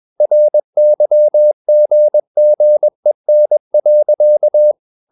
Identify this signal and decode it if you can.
Morse code, heard as a keyed tone, about 21 wpm, 600 Hz. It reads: RYGGEN.
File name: Ferie QRS 21wpm